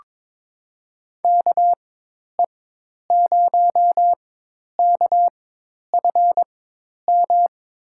Morse code, heard as a keyed tone, about 22 wpm, spaced as for 10 wpm, 700 Hz. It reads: KE0KFM
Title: USA Call Signs - 6 Characters - 4x Character Spacing - 22wpm